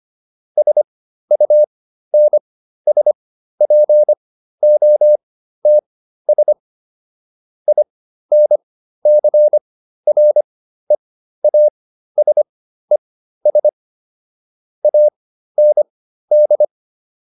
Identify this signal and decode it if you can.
Morse code, heard as a keyed tone, about 25 wpm, spaced as for 13 wpm, 600 Hz. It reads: SUNSPOTS INCREASES AND